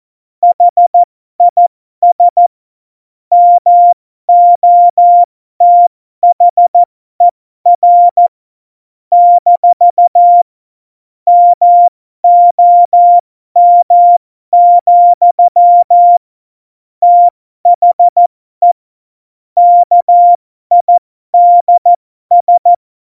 Morse code, heard as a keyed tone, about 14 wpm, spaced as for 12 wpm, 700 Hz. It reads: HIS MOTHER - MOM, THE KIDS
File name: Morsekurs / Witze